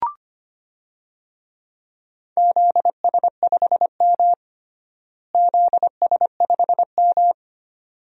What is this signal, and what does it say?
Morse code, heard as a keyed tone, about 25 wpm, 700 Hz. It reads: ZS5M ZS5M
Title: DXCC-100 Call Signs - 4 Characters - Repeated 2x - 25wpm